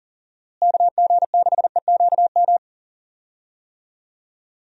Morse code, heard as a keyed tone, about 40 wpm, 700 Hz. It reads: KG6EQM
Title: USA Call Signs - 6 Characters - Repeated 3x - 40wpm